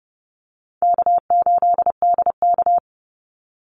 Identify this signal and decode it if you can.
Morse code, heard as a keyed tone, about 30 wpm, 700 Hz. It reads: K8DK
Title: Call Signs Repeated 2x - 30wpm